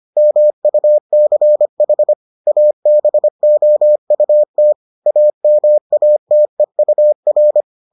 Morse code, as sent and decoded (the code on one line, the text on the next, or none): -- ..- -.-. .... / .- -... --- ..- - / .- -- .- - . ..- .-.
MUCH ABOUT AMATEUR